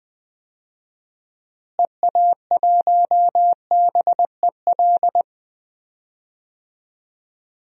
Morse code, heard as a keyed tone, about 20 wpm, 700 Hz. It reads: EA1BEL
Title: DXCC-100 Call Signs - 6 Characters - Repeated 2x - 20wpm